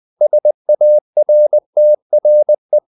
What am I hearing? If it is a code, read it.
Morse code, heard as a keyed tone, about 20 wpm, 600 Hz. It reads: SARTRE
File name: CW / Practice / EnglishQuotations0320